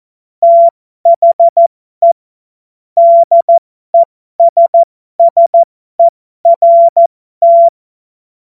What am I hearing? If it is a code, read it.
Morse code, heard as a keyed tone, about 14 wpm, spaced as for 12 wpm, 680 Hz. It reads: THE DESSERT
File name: Morsekurs / Witze